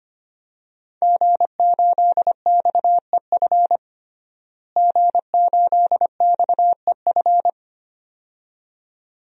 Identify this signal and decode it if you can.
Morse code, heard as a keyed tone, about 25 wpm, 700 Hz. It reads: G8XEF G8XEF
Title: DXCC-100 Call Signs - 5 Characters - Repeated 2x - 25wpm